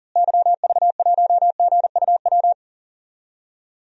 Morse code, heard as a keyed tone, about 39 wpm, 700 Hz. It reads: YV1GUW
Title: DXCC-100 Call Signs - 6 Characters - Repeated 3x - 40wpm